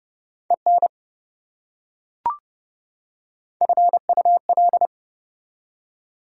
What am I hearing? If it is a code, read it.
Morse code, heard as a keyed tone, about 28 wpm, 700 Hz. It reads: EN FUL